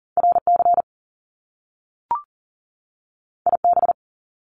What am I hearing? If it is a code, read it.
Morse code, heard as a keyed tone, about 40 wpm, 700 Hz. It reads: RC IB
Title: Top Two Letter Combinations in Words 40wpm